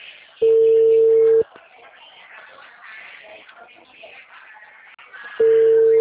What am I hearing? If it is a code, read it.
Ringback tone (China, 450 Hz), with speech.